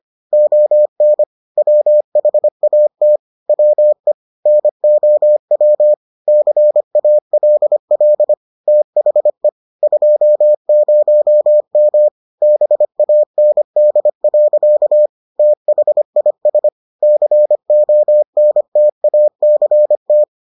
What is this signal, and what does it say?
Morse code, heard as a keyed tone, about 25 wpm, 600 Hz. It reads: ON WHAT WE NOW CALL THE 20M BAND. THIS CONTACT